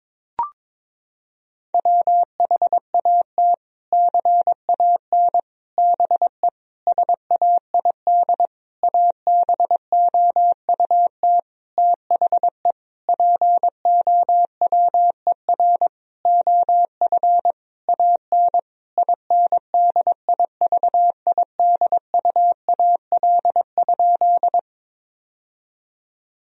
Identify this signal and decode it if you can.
Morse code, heard as a keyed tone, about 22 wpm, 700 Hz. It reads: WHAT CAN BE SAID ABOUT THE POWER OF AN INDIVIDUAL?
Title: Sentences from Top 1000 Words - Review Part 4 - 22wpm